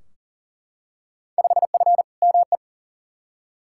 Morse code, heard as a keyed tone, about 40 wpm, 700 Hz. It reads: 5F ME
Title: Field Day Exchanges Encore - 40wpm